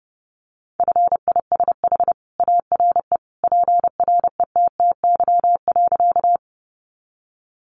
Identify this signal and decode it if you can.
Morse code, heard as a keyed tone, about 30 wpm, 700 Hz. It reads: FISH ARE PRETTY.
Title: Easy Phrases 30wpm